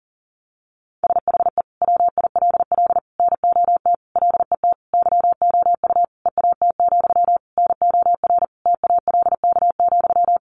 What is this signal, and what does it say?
Morse code, heard as a keyed tone, about 40 wpm, 700 Hz. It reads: SHE WILL NOT LET YOU EAT, NOR TALK,
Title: Sentences from Top 1500 Words 40wpm